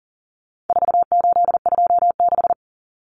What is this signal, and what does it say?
Morse code, heard as a keyed tone, about 40 wpm, 700 Hz. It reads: 4826
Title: Numbers - 4-Digit 40wpm